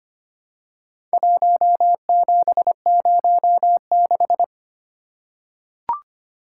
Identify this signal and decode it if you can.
Morse code, heard as a keyed tone, about 25 wpm, 700 Hz. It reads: 1706
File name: practice / 25wpm / Four Letter-Number Combinations - 25wpm